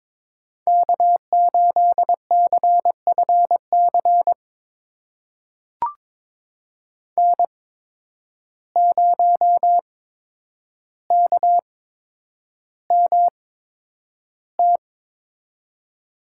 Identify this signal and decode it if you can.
Morse code, heard as a keyed tone, about 21 wpm, spaced as for 9 wpm, 700 Hz. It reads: K8CFC N0KMT
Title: USA Call Signs - 5 Characters - 8x Character Spacing - 22wpm